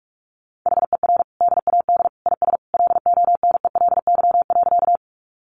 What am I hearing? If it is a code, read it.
Morse code, heard as a keyed tone, about 44 wpm, 700 Hz. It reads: HER DAD IS LONELY.
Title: Additional Easy Phrases 45wpm